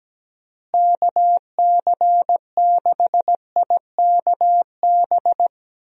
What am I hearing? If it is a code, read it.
Morse code, heard as a keyed tone, about 17 wpm, 700 Hz. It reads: KC6IKB
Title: USA Call Signs - 6 Characters - Repeated 3x - 17wpm